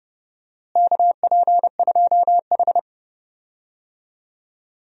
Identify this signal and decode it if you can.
Morse code, heard as a keyed tone, about 32 wpm, 700 Hz. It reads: KP2H
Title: DXCC-100 Call Signs - 4 Characters - Repeated 2x - 30wpm